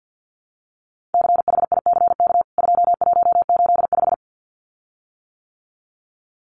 Morse code, heard as a keyed tone, about 48 wpm, 700 Hz. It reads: CHICK 2185
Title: CWT Exchange 50wpm